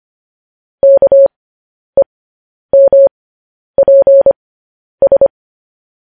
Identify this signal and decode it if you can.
Morse code, heard as a keyed tone, about 25 wpm, spaced as for 10 wpm, 550 Hz. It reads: KEMPS
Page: news-headlines-in-morse-code-at-10-wpm